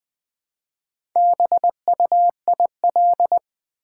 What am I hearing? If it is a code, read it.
Morse code, heard as a keyed tone, about 20 wpm, 700 Hz. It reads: BUIL
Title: Build A Word - 2 Letters At A Time - 8-Letter Words - 20wpm